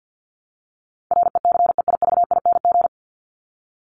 Morse code, heard as a keyed tone, about 48 wpm, 700 Hz. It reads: RECEIVING
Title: ING Suffix - Top 100 Words with Base Words 50wpm